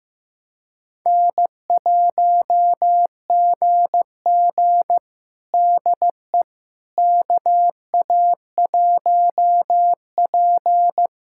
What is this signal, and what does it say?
Morse code, heard as a keyed tone, about 15 wpm, 700 Hz. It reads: N1GG DE KA1P
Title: USA Callsign DE Callsign - 4 Characters - 15wpm